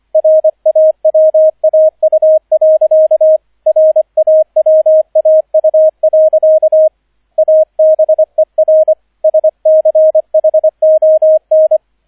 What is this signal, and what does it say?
Morse code, heard as a keyed tone, about 25 wpm, 630 Hz, with radio noise. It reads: RAWAU. RAWAU. ABER SCHON